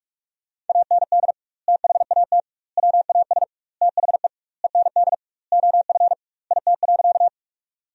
Morse code, heard as a keyed tone, about 46 wpm, 700 Hz. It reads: AND THAT WAS THE END OF IT.
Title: 2x Word-Spacing - Sentences from Top 200 Words - 45wpm